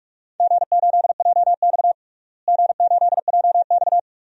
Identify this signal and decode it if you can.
Morse code, heard as a keyed tone, about 41 wpm, 700 Hz. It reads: G8JX G8JX
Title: DXCC-100 Call Signs - 4 Characters - Repeated 2x - 45wpm